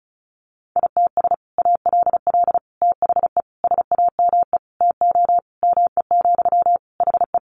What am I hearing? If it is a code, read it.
Morse code, heard as a keyed tone, about 34 wpm, 700 Hz. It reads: ITS ALL THE SAME TO ME, HE
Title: Sentences from Top 600 Words 35wpm